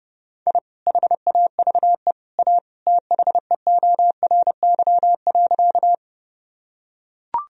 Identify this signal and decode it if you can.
Morse code, heard as a keyed tone, about 30 wpm, 700 Hz. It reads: I HAVE A THEORY.
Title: Sentences from Top 1000 Words - Review Part 3 - 30wpm